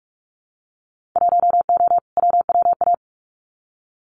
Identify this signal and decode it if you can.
Morse code, heard as a keyed tone, about 46 wpm, 700 Hz. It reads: 1O WWA
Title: Winter Field Day Exchanges 45wpm